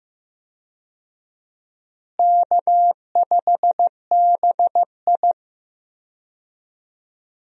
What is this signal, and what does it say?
Morse code, heard as a keyed tone, about 15 wpm, 700 Hz. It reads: K5BI
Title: USA Call Signs - 4 Characters - Repeated 2x - 15wpm